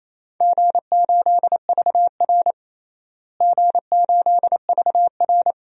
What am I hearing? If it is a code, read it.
Morse code, heard as a keyed tone, about 28 wpm, 700 Hz. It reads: G8VR G8VR
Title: DXCC-100 Call Signs - 4 Characters - Repeated 3x - 28wpm